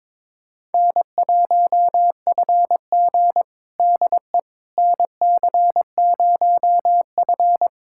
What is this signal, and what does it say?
Morse code, heard as a keyed tone, about 21 wpm, 700 Hz. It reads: N1FG DE NC0F
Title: USA Callsign DE Callsign - 4 Characters - 22wpm